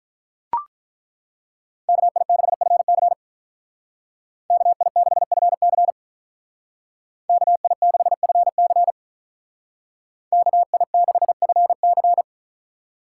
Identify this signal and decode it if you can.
Morse code, heard as a keyed tone, about 43 wpm, 700 Hz. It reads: KI6FC KI6FC KI6FC KI6FC
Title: Speed Racing - USA 2x2 Call Signs 35wpm